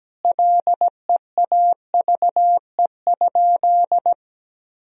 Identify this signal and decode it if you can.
Morse code, heard as a keyed tone, about 17 wpm, 700 Hz. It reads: LEAVE?
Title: Sentences from Top 1000 Words - Review Part 2 - 17wpm